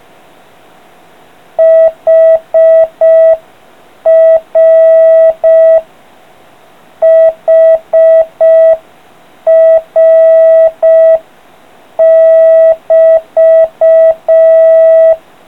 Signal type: Morse code